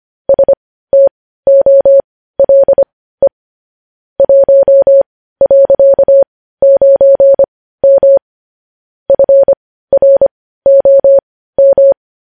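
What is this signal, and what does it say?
Morse code, heard as a keyed tone, about 25 wpm, spaced as for 15 wpm, 550 Hz. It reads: STOLE 1.9M FROM
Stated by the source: Morse code news headlines at 15 WPM.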